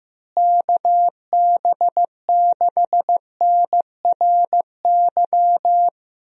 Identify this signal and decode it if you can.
Morse code, heard as a keyed tone, about 15 wpm, 700 Hz. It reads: KB6NRY
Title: USA Call Signs - 6 Characters - Repeated 3x - 15wpm